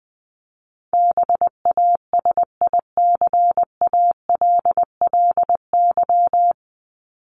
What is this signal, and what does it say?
Morse code, heard as a keyed tone, about 20 wpm, 700 Hz. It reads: BASICALLY
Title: Top 1251-1500 Words 20wpm